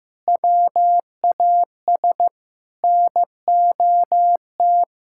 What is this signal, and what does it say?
Morse code, heard as a keyed tone, about 16 wpm, 700 Hz. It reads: WAS NOT
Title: Sentences from Top 1600 Words 15wpm